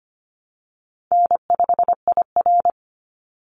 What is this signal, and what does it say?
Morse code, heard as a keyed tone, about 25 wpm, 700 Hz. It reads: N5IR
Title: Call Signs Repeated 2x - 25wpm